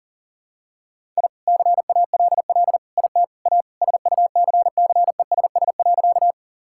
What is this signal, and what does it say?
Morse code, heard as a keyed tone, about 40 wpm, 700 Hz. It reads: I CALL IT A SUCCESS.
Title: Sentences from Top 1000 Words - Review Part 6 - 40wpm